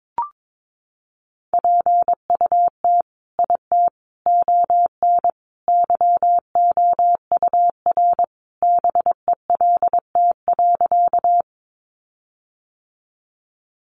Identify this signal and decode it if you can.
Morse code, heard as a keyed tone, about 22 wpm, 700 Hz. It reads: PUT IT ON YOUR BELT.
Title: Sentences from Top 1400 Words 22wpm